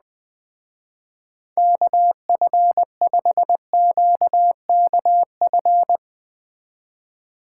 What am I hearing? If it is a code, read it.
Morse code, heard as a keyed tone, about 20 wpm, 700 Hz. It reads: KF5QKF
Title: Speed Racing - USA 2x3 Call Signs 17wpm